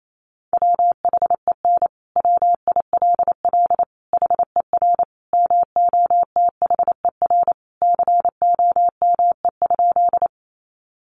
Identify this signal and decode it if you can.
Morse code, heard as a keyed tone, about 28 wpm, 700 Hz. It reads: WHEN WILL HER MOTHER COME?
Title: Sentences from Top 200 Words 28wpm